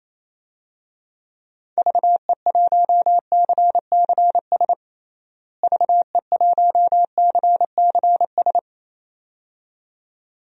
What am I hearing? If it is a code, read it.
Morse code, heard as a keyed tone, about 28 wpm, 700 Hz. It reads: VE1CCS VE1CCS
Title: DXCC-100 Call Signs - 6 Characters - Repeated 2x - 28wpm